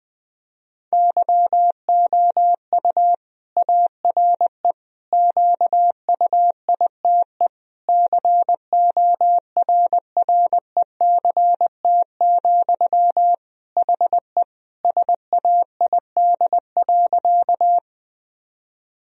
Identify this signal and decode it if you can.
Morse code, heard as a keyed tone, about 20 wpm, 700 Hz. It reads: YOU ARE QUITE CORRECT, HE SAID.
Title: Sentences from Top 1000 Words - Review Part 2 - 20wpm